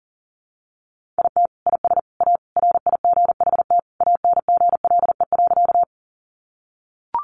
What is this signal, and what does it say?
Morse code, heard as a keyed tone, about 37 wpm, 700 Hz. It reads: IT IS A RIGHT ANGLE.
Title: Sentences from Top 1500 Words 40wpm